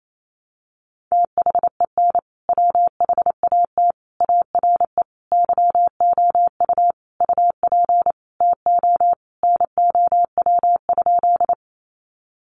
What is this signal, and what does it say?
Morse code, heard as a keyed tone, about 28 wpm, 700 Hz. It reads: THEN WHAT ARE YOU UP TO NOW?
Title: Sentences from Top 100 Words 28wpm